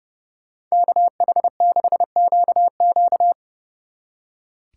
Morse code, heard as a keyed tone, about 36 wpm, 700 Hz. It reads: KH6QQ
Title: Speed Racing - USA 2x2 Call Signs 30wpm